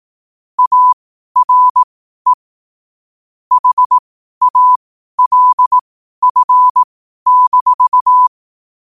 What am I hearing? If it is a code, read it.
Morse code, heard as a keyed tone, about 18 wpm, spaced as for 12 wpm, 1000 Hz. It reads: ARE HALF-
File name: qotdfiles / 12wpm